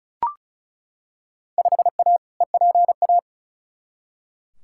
Morse code, heard as a keyed tone, about 34 wpm, 700 Hz. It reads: 5A EPA
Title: Field Day Exchanges - 35wpm